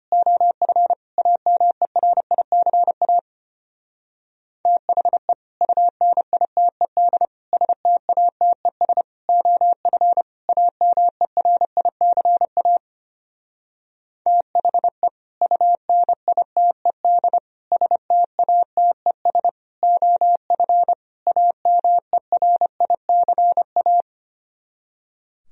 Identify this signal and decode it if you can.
Morse code, heard as a keyed tone, about 29 wpm, 700 Hz. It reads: OF AMERICA THE UNITED STATES OF AMERICA THE UNITED STATES OF AMERICA
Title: Speed-Racing - Sets of 5 Words 25wpm